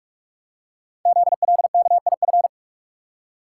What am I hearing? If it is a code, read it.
Morse code, heard as a keyed tone, about 44 wpm, 690 Hz. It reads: ZLKIF
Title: Code Groups - 5 Letters - 45wpm